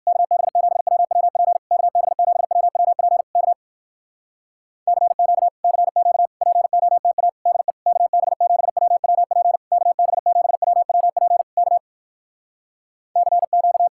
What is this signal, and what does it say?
Morse code, heard as a keyed tone, about 67 wpm, 700 Hz. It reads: KB7WWP KB7WWP K CQ CQ POTA DE KB7WWP KB7WWP K CQ